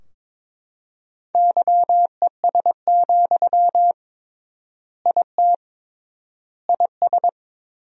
Morse code, heard as a keyed tone, about 22 wpm, 700 Hz. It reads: YES, IT IS
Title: 3x Word-Spacing - Sentences from Top 500 Words - 22wpm